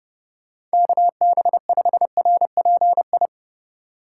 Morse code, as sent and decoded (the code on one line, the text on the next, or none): -.- -... ..... .-. .--. ..
KB5RPI